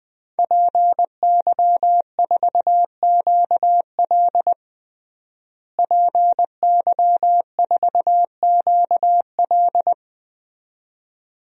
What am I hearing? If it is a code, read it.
Morse code, heard as a keyed tone, about 20 wpm, 700 Hz. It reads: PY4QL PY4QL
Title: DXCC-100 Call Signs - 5 Characters - Repeated 2x - 20wpm